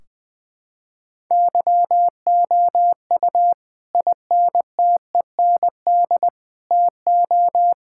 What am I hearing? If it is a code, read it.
Morse code, heard as a keyed tone, about 20 wpm, 700 Hz. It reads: YOU INTEND TO
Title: Sentences from Top 1600 Words 20wpm